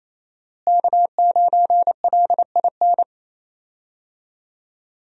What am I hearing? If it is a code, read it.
Morse code, heard as a keyed tone, about 28 wpm, 700 Hz. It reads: K9LIN
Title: USA Call Signs - 5 Characters - Repeated 3x - 28wpm